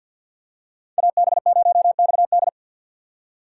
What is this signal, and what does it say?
Morse code, heard as a keyed tone, about 52 wpm, 700 Hz. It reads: AB0XD
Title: USA Call Signs - 5 Characters - 2x Character Spacing - 50wpm